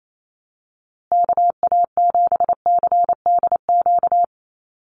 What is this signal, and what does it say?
Morse code, heard as a keyed tone, about 28 wpm, 700 Hz. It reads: KA7CDQ
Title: USA Call Signs Repeated 2x 28wpm